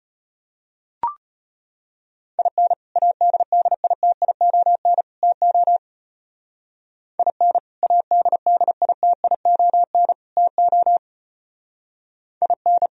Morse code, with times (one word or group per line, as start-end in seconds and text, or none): 2.4-2.7 IN
3.0-5.0 ADDITION
5.2-5.8 TO
7.2-7.6 IN
7.8-10.1 ADDITION
10.4-11.0 TO
12.4-12.9 IN